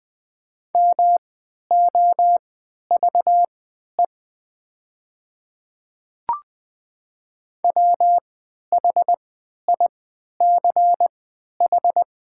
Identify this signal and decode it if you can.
Morse code, heard as a keyed tone, about 20 wpm, spaced as for 11 wpm, 700 Hz. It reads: MOVE WHICH
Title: 3x Character Spacing - Top 300 Words - 20wpm